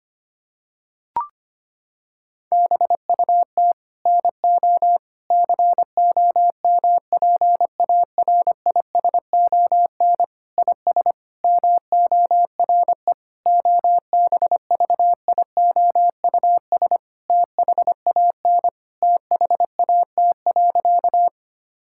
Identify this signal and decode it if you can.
Morse code, heard as a keyed tone, about 25 wpm, 700 Hz. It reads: BUT NO COMPARISON IS MORE OBVIOUS THAN THAT.
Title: Sentences from Top 1600 Words 25wpm